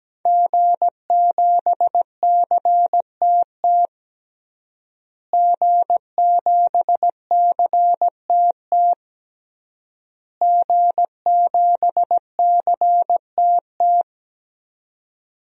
Morse code, heard as a keyed tone, about 17 wpm, 700 Hz. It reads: G7CTT G7CTT G7CTT
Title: DXCC-100 Call Signs - 5 Characters - Repeated 3x - 17wpm